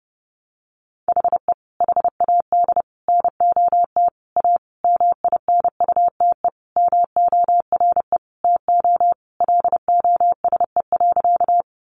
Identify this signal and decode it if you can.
Morse code, heard as a keyed tone, about 30 wpm, 700 Hz. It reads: HE HAD NOT A MINUTE MORE TO LOSE.
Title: Sentences from Top 1100 Words 30wpm